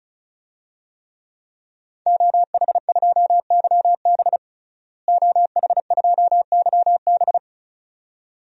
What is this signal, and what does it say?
Morse code, heard as a keyed tone, about 34 wpm, 700 Hz. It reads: OH2YB OH2YB
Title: DXCC-100 Call Signs - 5 Characters - Repeated 2x - 35wpm